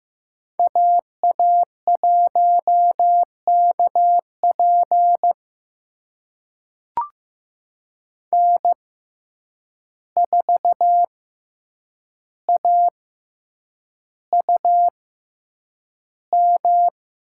Morse code, heard as a keyed tone, about 15 wpm, spaced as for 8 wpm, 700 Hz. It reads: AA1KP N4AUM
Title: USA Call Signs - 5 Characters - 6x Character Spacing - 15wpm